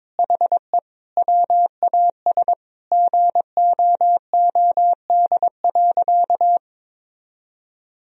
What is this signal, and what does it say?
Morse code, heard as a keyed tone, about 21 wpm, 700 Hz. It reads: HE WAS GOOD.
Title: Sentences from Top 1000 Words - Review Part 2 - 22wpm